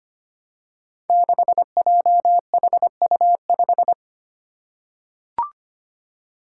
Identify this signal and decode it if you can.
Morse code, heard as a keyed tone, about 25 wpm, 700 Hz. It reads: 6JHU5
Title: Five Challenging Letter-Number Combinations 25wpm